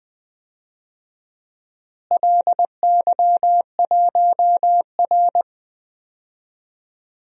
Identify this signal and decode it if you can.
Morse code, heard as a keyed tone, about 20 wpm, 700 Hz. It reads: LY1R
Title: DXCC-100 Call Signs - 4 Characters - Repeated 2x - 20wpm